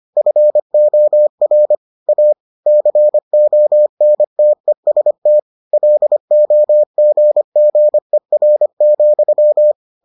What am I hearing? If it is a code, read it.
Morse code, heard as a keyed tone, about 25 wpm, 600 Hz. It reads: FOR A CONTEST LOGGER,